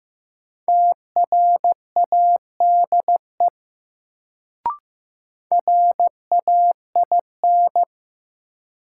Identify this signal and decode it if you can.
Morse code, heard as a keyed tone, about 15 wpm, 700 Hz. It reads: TRADE RAIN
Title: Sentences from Top 600 Words - New Word Drill - 15wpm